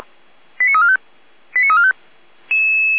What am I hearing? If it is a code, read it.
Five-tone selective call (ZVEI), heard as quick runs of five tones, tones of about 70 ms. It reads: 89345 89345